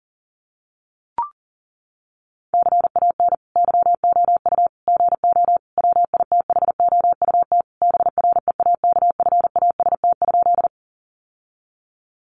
Morse code, heard as a keyed tone, about 37 wpm, 700 Hz. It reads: CAN YOU GO WITHOUT BREAKFAST?
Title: Sentences from Top 1500 Words 40wpm